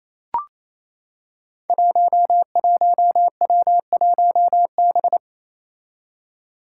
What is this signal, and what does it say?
Morse code, heard as a keyed tone, about 28 wpm, 700 Hz. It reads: 11W1B
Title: Five Challenging Letter-Number Combinations 28wpm